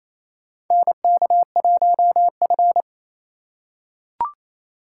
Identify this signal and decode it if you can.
Morse code, heard as a keyed tone, about 28 wpm, 700 Hz. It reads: NK1F
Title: USA Call Signs - 4 Characters - 3x Character Spacing - 28wpm